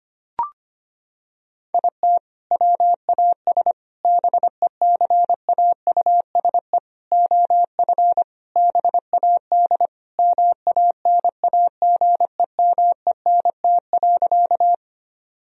Morse code, heard as a keyed tone, about 25 wpm, 690 Hz. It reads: IT WAS BECAUSE OF BAD MANAGEMENT.
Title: Sentences from Top 1000 Words - Review Part 4 - 25wpm